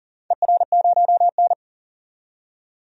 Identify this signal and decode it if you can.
Morse code, heard as a keyed tone, about 40 wpm, 700 Hz. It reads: ER0N
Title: DXCC-100 Call Signs - 4 Characters - 40wpm